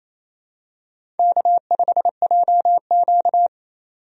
Morse code, heard as a keyed tone, about 28 wpm, 700 Hz. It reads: K5JQ
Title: USA Call Signs - 4 Characters - 28wpm